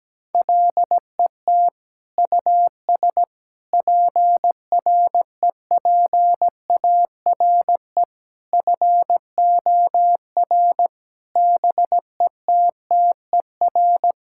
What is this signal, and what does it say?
Morse code, heard as a keyed tone, about 17 wpm, 700 Hz. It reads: LET US PREPARE FOR BETTER
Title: Sentences from Top 1000 Words - Review Part 6 - 17wpm